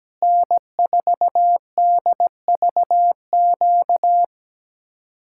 Morse code, heard as a keyed tone, about 17 wpm, 700 Hz. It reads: N4DVQ
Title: Speed-Racing Common QSO Phrases Encore - 17wpm